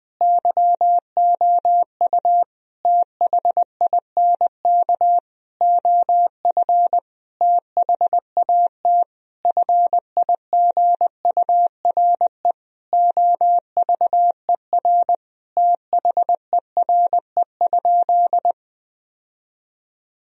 Morse code, heard as a keyed tone, about 20 wpm, 700 Hz. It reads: YOU THINK OF THAT FIGURE OVER THERE?
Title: Sentences from Top 1000 Words - Review Part 2 - 20wpm